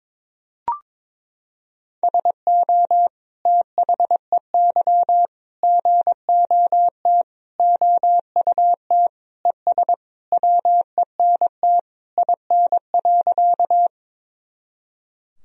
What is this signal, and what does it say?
Morse code, heard as a keyed tone, about 22 wpm, 700 Hz. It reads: SO THEY GOT OUT ES WENT IN.
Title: Sentences from Top 300 Words and Abbreviations 22wpm